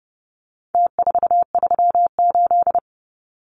Morse code, heard as a keyed tone, about 30 wpm, 700 Hz. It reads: T438
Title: Numbers - Contest Style - 4-Digit 30wpm